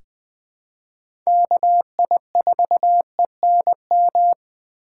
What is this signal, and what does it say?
Morse code, heard as a keyed tone, about 20 wpm, 700 Hz. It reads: KI4ENM
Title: USA Call Signs - 6x Character Spacing - 20wpm